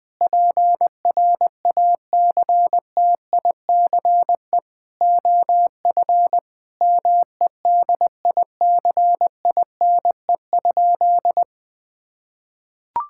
Morse code, heard as a keyed tone, about 20 wpm, 700 Hz. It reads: PRACTICE OF MEDICINE?
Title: Sentences from Top 1000 Words - Review Part 6 - 20wpm